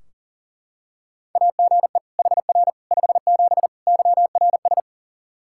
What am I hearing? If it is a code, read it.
Morse code, heard as a keyed tone, about 40 wpm, 700 Hz. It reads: AGE HR 57 YRS
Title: Practice QSO - Exchange 3 - More Info - 40wpm